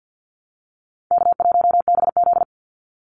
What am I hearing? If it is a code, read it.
Morse code, heard as a keyed tone, about 50 wpm, 700 Hz. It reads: K1BZ
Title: Call Signs Repeated 2x - 50wpm